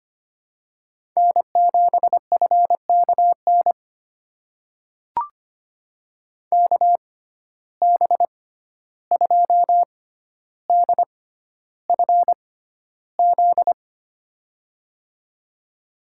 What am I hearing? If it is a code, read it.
Morse code, heard as a keyed tone, about 25 wpm, spaced as for 12 wpm, 700 Hz. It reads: N7FKN KB2DFZ
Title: USA Call Signs - 6x Character Spacing - 25wpm